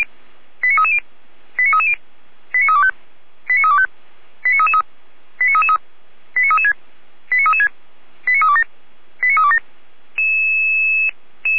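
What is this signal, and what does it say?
Five-tone selective call (ZVEI), heard as quick runs of five tones, tones of about 70 ms. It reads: 89330 89330 89325 89325 89333 89336 89336 89327 89327
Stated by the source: Five-tone ZVEI selective call 89319 received by SDR, archived by Feuerwehr Orlamünde fire brigade.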